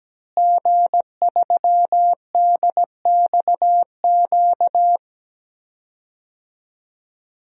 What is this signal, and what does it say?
Morse code, heard as a keyed tone, about 17 wpm, 700 Hz. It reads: G3DXQ
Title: DXCC-100 Call Signs - 5 Characters - Repeated 3x - 17wpm